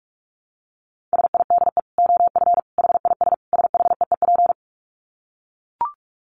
Morse code, heard as a keyed tone, about 44 wpm, 700 Hz. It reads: SIDE OF HIS SHEEP
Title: Dit Challenge 45wpm